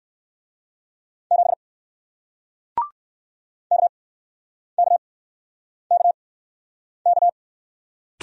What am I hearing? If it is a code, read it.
Morse code, heard as a keyed tone, about 59 wpm, spaced as for 9 wpm, 700 Hz. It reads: 7 KKKK